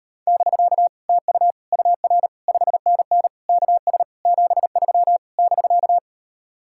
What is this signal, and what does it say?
Morse code, heard as a keyed tone, about 38 wpm, 700 Hz. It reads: <BK> TU UR 5NN KS 73 <BK>